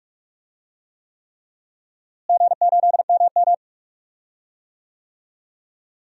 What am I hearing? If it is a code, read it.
Morse code, heard as a keyed tone, about 46 wpm, 700 Hz. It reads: G8MM
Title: DXCC-100 Call Signs - 4 Characters - Repeated 3x - 45wpm